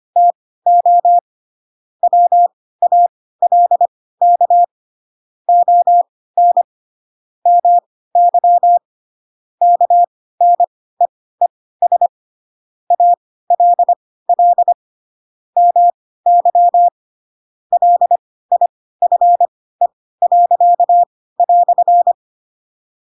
Morse code, heard as a keyed tone, about 25 wpm, spaced as for 16 wpm, 700 Hz. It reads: TO WALK ON MY KNEES ALL MY LIFE."
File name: Pinocchio / Pinocchio ch7 10 25 ik1hge